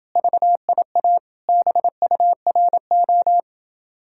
Morse code, heard as a keyed tone, about 29 wpm, 700 Hz. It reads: VIA BURO